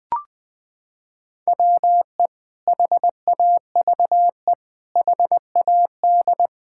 Morse code, heard as a keyed tone, about 20 wpm, 700 Hz. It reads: WE HAVE HAD
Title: Sentences from Top 1000 Words - Review Part 5 - 20wpm